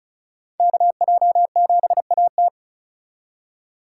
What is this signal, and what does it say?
Morse code, heard as a keyed tone, about 45 wpm, 700 Hz. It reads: KJ7AT